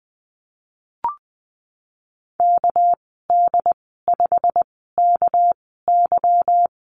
Morse code, heard as a keyed tone, about 20 wpm, spaced as for 15 wpm, 700 Hz. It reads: KD5KY
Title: 213 Lesson 34g - Number 8 - Call Signs 20wpm